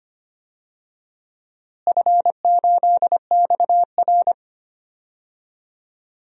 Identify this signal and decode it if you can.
Morse code, heard as a keyed tone, about 25 wpm, 700 Hz. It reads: F8XR
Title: DXCC-100 Call Signs - 4 Characters - Repeated 2x - 25wpm